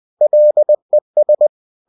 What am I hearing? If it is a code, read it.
Morse code, heard as a keyed tone, about 20 wpm, 600 Hz. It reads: LES